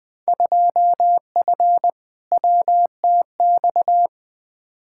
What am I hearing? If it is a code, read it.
Morse code, heard as a keyed tone, about 20 wpm, 700 Hz. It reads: 2F WTX
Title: Field Day Exchanges - 20wpm